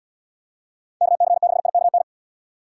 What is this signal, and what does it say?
Morse code, heard as a keyed tone, about 77 wpm, 700 Hz. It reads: KC6IKN